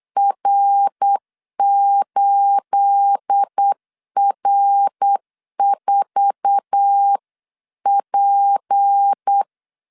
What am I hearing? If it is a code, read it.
Morse code, heard as a keyed tone, about 8 wpm, 800 Hz. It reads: R8R4P
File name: CW / DET / 1 9 1500C S B 8wpm 5624